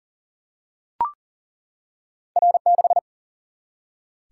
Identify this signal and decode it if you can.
Morse code, heard as a keyed tone, about 42 wpm, 700 Hz. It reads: R 6
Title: Single-Letter Single-Number 40wpm